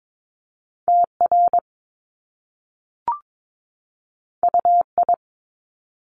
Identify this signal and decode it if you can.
Morse code, heard as a keyed tone, about 22 wpm, 700 Hz. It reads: TR UI
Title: Top Two Letter Combinations in Words 22wpm